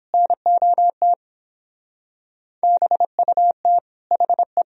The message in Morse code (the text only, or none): NOT BUT HE